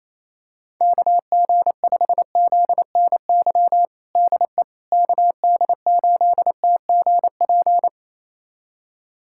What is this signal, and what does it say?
Morse code, heard as a keyed tone, about 27 wpm, 700 Hz. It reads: KG5ZNY DE KD8TGP
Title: USA Callsign DE Callsign - 6 Characters - 28wpm